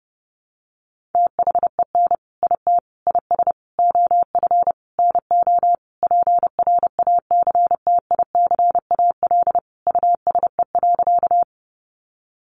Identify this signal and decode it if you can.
Morse code, heard as a keyed tone, about 30 wpm, 700 Hz. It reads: THEN IT IS OF NO PRACTICAL USE.
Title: Sentences from Top 1400 Words 30wpm